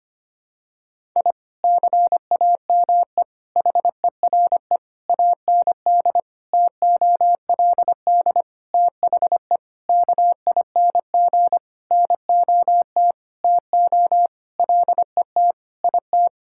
Morse code, heard as a keyed tone, about 25 wpm, 700 Hz. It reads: I CAME HERE AND TOLD THE KING NOT TO LET IT
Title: Sentences from Top 1000 Words - Review Part 6 - 25wpm